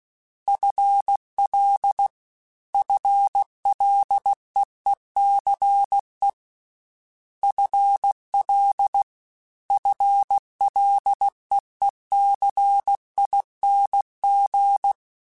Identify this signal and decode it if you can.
Morse code, heard as a keyed tone, about 16 wpm, 800 Hz. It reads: FL FLEECE FL FLEECING
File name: FL / FL mp3s / FL Prefixes 1x1 @15